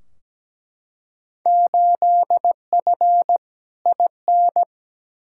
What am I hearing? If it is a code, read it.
Morse code, heard as a keyed tone, about 17 wpm, 700 Hz. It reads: 8F IN
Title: Field Day Exchanges Encore - 17wpm